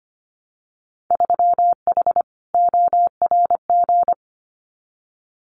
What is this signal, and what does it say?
Morse code, heard as a keyed tone, about 25 wpm, 700 Hz. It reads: 3H ORG
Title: Winter Field Day Exchanges Encore 25wpm